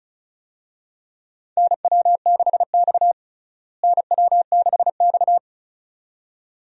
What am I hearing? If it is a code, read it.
Morse code, heard as a keyed tone, about 36 wpm, 700 Hz. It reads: NW6X NW6X
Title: USA Call Signs - 4 Characters - Repeated 2x - 35wpm